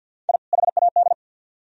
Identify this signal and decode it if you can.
Morse code, heard as a keyed tone, about 60 wpm, 700 Hz. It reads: I HAD
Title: Speed-Racing - Sets of 2 Words 50wpm